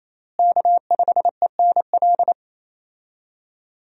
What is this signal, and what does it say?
Morse code, heard as a keyed tone, about 28 wpm, 700 Hz. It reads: K5ENL
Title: USA Call Signs - 5 Characters - Repeated 3x - 28wpm